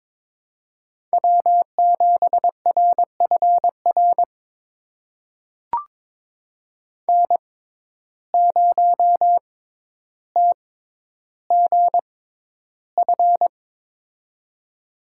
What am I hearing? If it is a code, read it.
Morse code, heard as a keyed tone, about 22 wpm, spaced as for 11 wpm, 700 Hz. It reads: W7RFR N0TGF
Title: USA Call Signs - 5 Characters - 6x Character Spacing - 22wpm